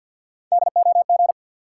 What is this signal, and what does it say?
Morse code, heard as a keyed tone, about 53 wpm, 700 Hz. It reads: DOG
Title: Three Letter Words - 50wpm